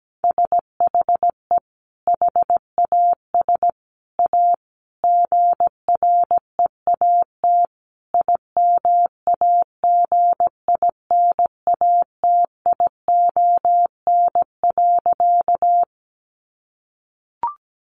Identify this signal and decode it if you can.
Morse code, heard as a keyed tone, about 17 wpm, 700 Hz. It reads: SHE HAS A GREAT IMAGINATION.
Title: Sentences from Top 1300 Words 17wpm